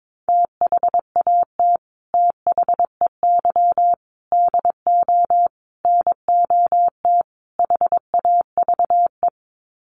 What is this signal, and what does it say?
Morse code, heard as a keyed tone, about 22 wpm, 700 Hz. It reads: THAT THEY DO NOT HAVE
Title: Sets of 5 Words Encore2 22wpm